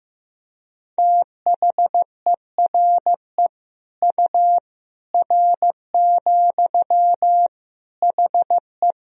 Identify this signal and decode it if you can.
Morse code, heard as a keyed tone, about 15 wpm, 700 Hz. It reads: THERE U R, HE
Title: Sentences from Top 100 Words and Abbreviations 15wpm